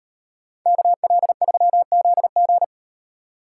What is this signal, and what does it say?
Morse code, heard as a keyed tone, about 36 wpm, 700 Hz. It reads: KL3ZG